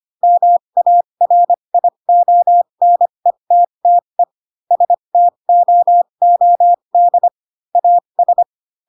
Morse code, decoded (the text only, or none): MARIONETTE STOOD AS